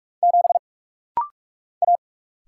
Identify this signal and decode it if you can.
Morse code, heard as a keyed tone, about 44 wpm, 700 Hz. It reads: 7 A
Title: Single Letter-Number - Rapid-Fire 45wpm